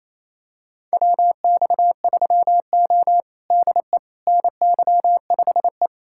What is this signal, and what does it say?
Morse code, heard as a keyed tone, about 28 wpm, 700 Hz. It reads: WX3O DE NY5E
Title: USA Callsign DE Callsign - 4 Characters - 28wpm